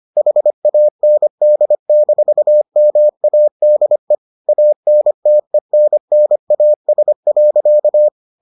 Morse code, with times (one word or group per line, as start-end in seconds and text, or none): 0.2-4.2 HAND-MADE
4.5-8.1 ANTENNAS.